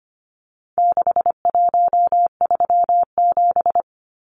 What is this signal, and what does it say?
Morse code, heard as a keyed tone, about 25 wpm, 700 Hz. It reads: 6137
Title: Numbers - 4-Digit 25wpm